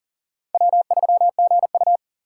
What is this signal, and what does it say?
Morse code, heard as a keyed tone, about 40 wpm, 700 Hz. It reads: W3GU
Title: USA Call Signs - 4 Characters - 4x Character Spacing - 40wpm